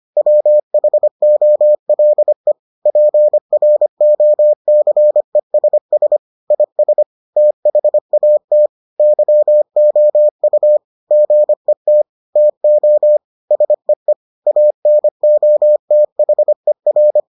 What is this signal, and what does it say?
Morse code, heard as a keyed tone, about 25 wpm, 600 Hz. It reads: WHOLE PROCESS IS THAT YOU GET TO SEE ANOTHER